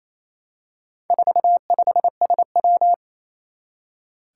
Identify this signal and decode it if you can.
Morse code, heard as a keyed tone, about 28 wpm, 700 Hz. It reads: 45SW
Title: Four Challenging Letter-Number Combinations 28wpm